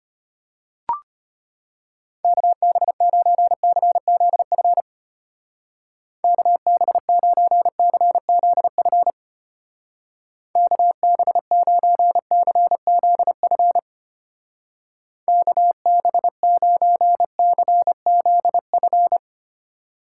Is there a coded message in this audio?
Yes: Morse code from 2.2 to 19.2 s, heard as a keyed tone, about 32 wpm, 700 Hz. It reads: KB9CZF KB9CZF KB9CZF KB9CZF